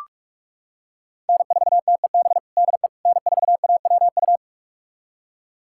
Morse code, heard as a keyed tone, about 46 wpm, 700 Hz. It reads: N4TEB DE N4AWU
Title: USA Callsign DE Callsign - 5 Characters - 45wpm